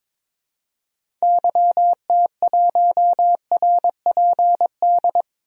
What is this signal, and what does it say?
Morse code, heard as a keyed tone, about 21 wpm, 700 Hz. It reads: YT1RPD
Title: DXCC-100 Call Signs - 6 Characters - 22wpm